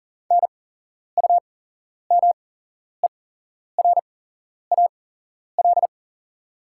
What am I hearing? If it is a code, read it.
Morse code, heard as a keyed tone, about 40 wpm, spaced as for 11 wpm, 700 Hz. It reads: NUMERAL
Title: 8x Character Spacing - Top 500 Words - 40wpm